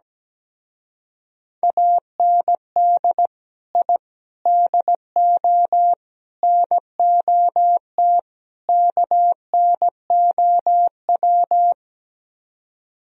Morse code, heard as a keyed tone, about 17 wpm, 700 Hz. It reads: AND I DO NOT KNOW
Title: Speed-Racing - Sets of 5 Words 17wpm